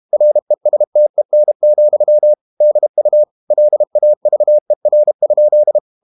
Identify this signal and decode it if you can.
Morse code, heard as a keyed tone, about 32 wpm, 600 Hz. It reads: RESTEN, DU LAVER?
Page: Ferie 32 wpm